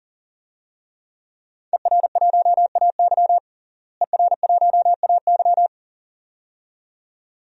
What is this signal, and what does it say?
Morse code, heard as a keyed tone, about 37 wpm, 700 Hz. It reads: ER1AY ER1AY
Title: DXCC-100 Call Signs - 5 Characters - Repeated 2x - 40wpm